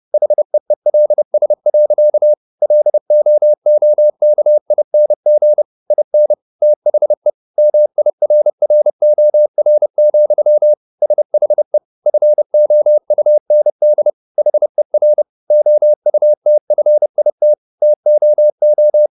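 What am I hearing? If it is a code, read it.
Morse code, heard as a keyed tone, about 30 wpm, 600 Hz. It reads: HEELS. LOOKING IN THE MIRROR, SHE FOUND HER OUTFIT TOO
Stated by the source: Morse code at 30 wpm, the novel Never by Ken Follett in English, from CW med Gnister.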